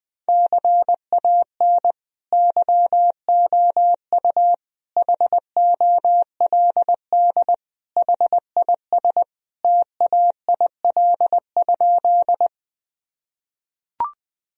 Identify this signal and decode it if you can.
Morse code, heard as a keyed tone, about 20 wpm, 700 Hz. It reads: CAN YOU HOLD HIS TAIL?
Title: Sentences from Top 1000 Words - Review Part 3 - 20wpm